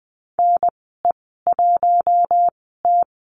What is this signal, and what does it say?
Morse code, heard as a keyed tone, about 20 wpm, spaced as for 14 wpm, 700 Hz. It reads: NE1T
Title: 042 Lesson 8g - Number 1 - Call Signs 20wpm